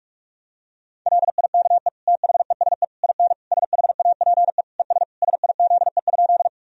Morse code, heard as a keyed tone, about 42 wpm, 700 Hz. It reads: LIKE THESE IN SHAPE ES SIZE?
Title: Sentences from Top 500 Words and Abbreviations 45wpm